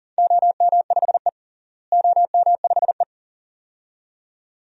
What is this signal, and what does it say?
Morse code, heard as a keyed tone, about 40 wpm, 700 Hz. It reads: OM5E OM5E
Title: DXCC-100 Call Signs - 4 Characters - Repeated 2x - 40wpm